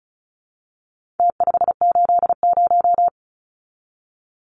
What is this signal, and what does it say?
Morse code, heard as a keyed tone, about 34 wpm, 700 Hz. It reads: T580
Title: Numbers - Contest Style - 4-Digit 35wpm